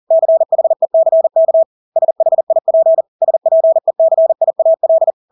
Morse code, heard as a keyed tone, about 40 wpm, 650 Hz. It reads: CHECK SHIP SPECIAL
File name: Engelske ord 40-40 wpm